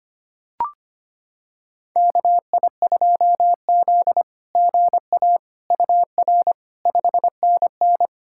Morse code, heard as a keyed tone, about 25 wpm, 700 Hz. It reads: KI2Z GA UR 5NN
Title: SOTA QSO Practice 25wpm